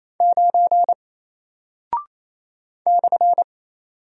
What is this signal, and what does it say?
Morse code, heard as a keyed tone, about 28 wpm, 700 Hz. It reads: 9 /
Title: Single Character (Letter:Number:Punctuation:Prosign) 28wpm